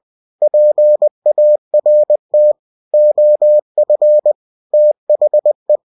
Morse code, heard as a keyed tone, about 20 wpm, 600 Hz. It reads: PART OF THE
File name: cw-books / winnie-the-pooh / winnie-20wpm-page-0087